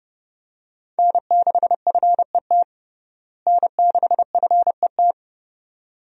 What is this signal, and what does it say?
Morse code, heard as a keyed tone, about 30 wpm, 700 Hz. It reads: N6FET N6FET
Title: USA Call Signs - 5 Characters - Repeated 2x - 30wpm